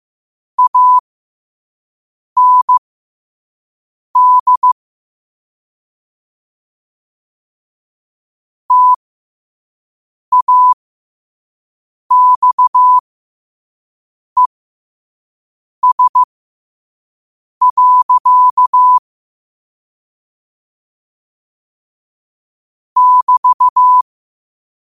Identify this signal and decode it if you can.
Morse code, heard as a keyed tone, about 15 wpm, spaced as for 5 wpm, 1000 Hz. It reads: AND TAXES. <BT>
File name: qotdfiles / 5wpm